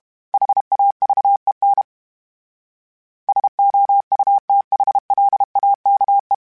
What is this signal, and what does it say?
Morse code, heard as a keyed tone, about 32 wpm, 800 Hz. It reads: HAVEN SOUTHLAKE
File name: fromSMs / FromSMs mp3s / Cities@30